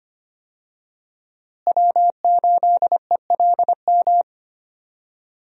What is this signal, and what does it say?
Morse code, heard as a keyed tone, about 25 wpm, 700 Hz. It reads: W8ELM
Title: USA Call Signs - 5 Characters - Repeated 2x - 25wpm